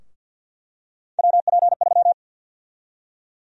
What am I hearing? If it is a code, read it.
Morse code, heard as a keyed tone, about 50 wpm, 700 Hz. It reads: WP3
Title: USA Call Sign Prefixes - 50wpm